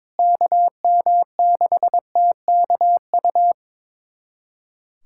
Morse code, heard as a keyed tone, about 22 wpm, 700 Hz. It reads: KM6TKU
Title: USA Callsign DE Callsign 22wpm